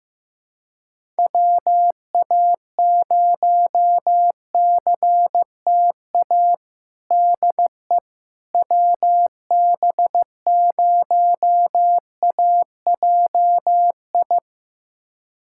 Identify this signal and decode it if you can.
Morse code, heard as a keyed tone, about 15 wpm, 700 Hz. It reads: WA0CTA DE WB0AJI
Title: USA Callsign DE Callsign - 6 Characters - 15wpm